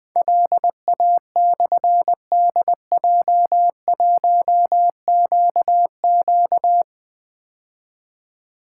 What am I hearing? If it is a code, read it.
Morse code, heard as a keyed tone, about 20 wpm, 700 Hz. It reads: LA/DJ1QQ
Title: Difficult Call Signs 20wpm